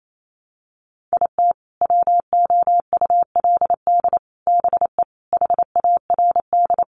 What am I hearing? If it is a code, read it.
Morse code, heard as a keyed tone, about 28 wpm, 700 Hz. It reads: IT WOULD BE HARD
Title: Sets of 5 Words Encore2 28wpm